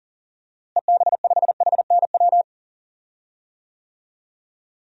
Morse code, heard as a keyed tone, about 40 wpm, 700 Hz. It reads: EB5HNW
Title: DXCC-100 Call Signs - 6 Characters - Repeated 2x - 40wpm